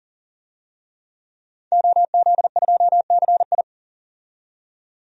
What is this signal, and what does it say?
Morse code, heard as a keyed tone, about 40 wpm, 700 Hz. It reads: OZ2CI
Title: DXCC-100 Call Signs - 5 Characters - 40wpm